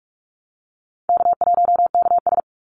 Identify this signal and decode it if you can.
Morse code, heard as a keyed tone, about 44 wpm, 700 Hz. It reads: K1KS
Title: Easy Call Signs Repeated 2x - 45wpm